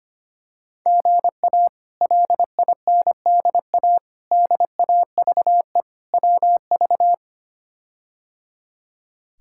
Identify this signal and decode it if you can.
Morse code, heard as a keyed tone, about 25 wpm, 700 Hz. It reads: GA LINDA DAVE WV
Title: K1USN SST QSO Practice - 25wpm